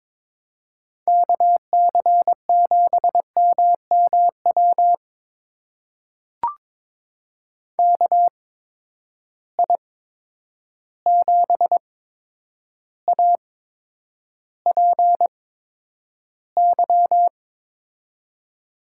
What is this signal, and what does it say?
Morse code, heard as a keyed tone, about 22 wpm, spaced as for 9 wpm, 700 Hz. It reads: KC7MMW KI7APY
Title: USA Call Signs - 6 Characters - 8x Character Spacing - 22wpm